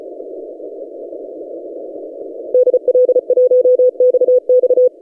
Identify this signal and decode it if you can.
Morse code, heard as a keyed tone, about 35 wpm, 500 Hz. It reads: DL1XX
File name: posts / 5j0r-san-andres-17cw